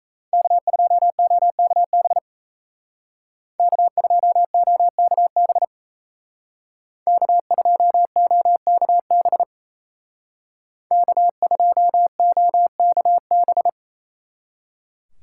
Morse code, heard as a keyed tone, about 36 wpm, 700 Hz. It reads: K2OKB K2OKB K2OKB K2OKB